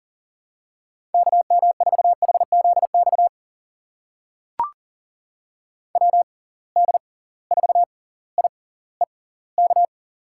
Morse code, heard as a keyed tone, about 39 wpm, spaced as for 20 wpm, 700 Hz. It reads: KM4HZX WD4IEK
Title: USA Call Signs - 6 Characters - 6x Character Spacing - 40wpm